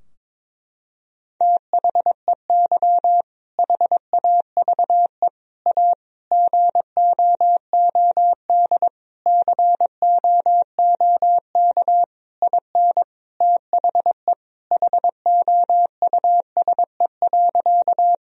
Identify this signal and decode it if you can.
Morse code, heard as a keyed tone, about 22 wpm, 700 Hz. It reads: THEY HAVE A GOOD COOK IN THE HOUSE.
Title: Sentences from Top 1000 Words - Review Part 4 - 22wpm